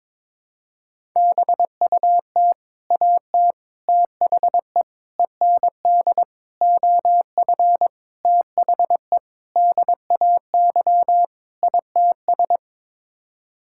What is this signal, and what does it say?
Morse code, heard as a keyed tone, about 22 wpm, 700 Hz. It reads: BUT AT THE END OF THE DAY ITS
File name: practice / 22wpm / Sets of 8 Words - 22wpm